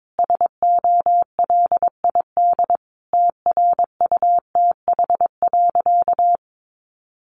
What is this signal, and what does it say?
Morse code, heard as a keyed tone, about 22 wpm, 700 Hz. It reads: SOLID TRUTH.
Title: Sentences from Top 1500 Words 22wpm